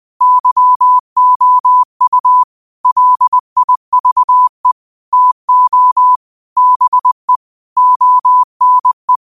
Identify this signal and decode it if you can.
Morse code, heard as a keyed tone, about 20 wpm, 1000 Hz. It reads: YOU LIVE TO BE ONE